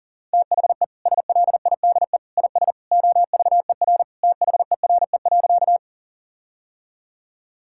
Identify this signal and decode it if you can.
Morse code, heard as a keyed tone, about 40 wpm, 700 Hz. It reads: THE SLIDE IS OVER THERE.
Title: Sentences from Top 1600 Words 40wpm